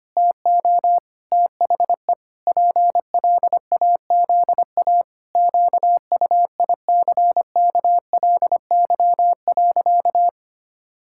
Morse code, heard as a keyed tone, about 25 wpm, 700 Hz. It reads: TO THE PLAZA QUICKLY.
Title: Pangram Challenge v2 - 25wpm